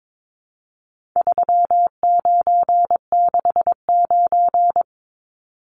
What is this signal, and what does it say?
Morse code, heard as a keyed tone, about 22 wpm, 700 Hz. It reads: 3969
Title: Numbers - 4-Digit 22wpm